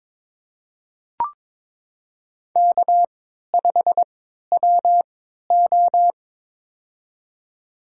Morse code, heard as a keyed tone, about 21 wpm, spaced as for 12 wpm, 700 Hz. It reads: K5WO
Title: USA Call Signs - 4 Characters - 3x Character Spacing - 22wpm